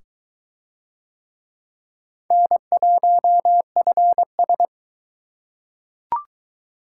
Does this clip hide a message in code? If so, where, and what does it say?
Yes: Morse code from 2.3 to 4.7 s, heard as a keyed tone, about 20 wpm, 700 Hz. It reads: N1FS